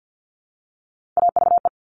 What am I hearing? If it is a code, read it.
Morse code, heard as a keyed tone, about 50 wpm, 690 Hz. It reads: AVE
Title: Top Three Letter Combinations in Words 50wpm